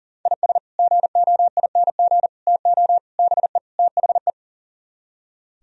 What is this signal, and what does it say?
Morse code, heard as a keyed tone, about 52 wpm, 700 Hz. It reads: IS GOING TO BE THE